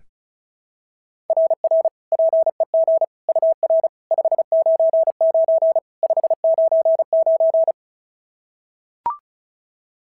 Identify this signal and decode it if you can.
Morse code, heard as a keyed tone, about 36 wpm, 690 Hz. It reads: RR PEG UR 599 599
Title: SOTA QSO Practice 35wpm